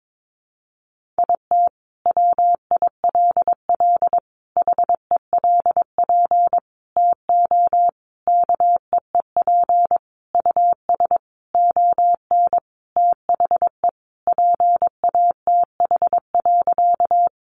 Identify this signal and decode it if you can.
Morse code, heard as a keyed tone, about 22 wpm, 700 Hz. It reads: IT WILL HELP TO KEEP US ON THE PATH.
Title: Sentences from Top 1400 Words 22wpm